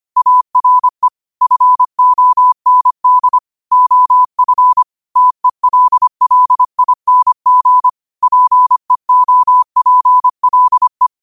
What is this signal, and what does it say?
Morse code, heard as a keyed tone, about 25 wpm, 1000 Hz. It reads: ARE FOND OF TELLING PEOPLE
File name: qotdfiles / 25wpm